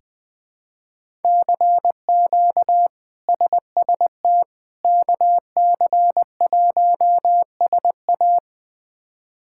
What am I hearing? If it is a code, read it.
Morse code, heard as a keyed tone, about 20 wpm, 700 Hz. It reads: CQ SST KC1SA
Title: K1USN SST QSO Practice - 20wpm